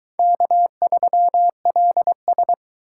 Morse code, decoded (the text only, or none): K3LS